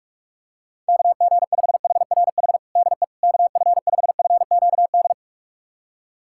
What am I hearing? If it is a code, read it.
Morse code, heard as a keyed tone, about 47 wpm, 700 Hz. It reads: KG5HRH DE KF5FQD
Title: USA Callsign DE Callsign - 6 Characters - 45wpm